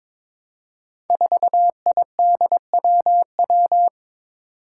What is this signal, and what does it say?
Morse code, heard as a keyed tone, about 21 wpm, 700 Hz. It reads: 4IDWW
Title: Five Challenging Letter-Number Combinations 22wpm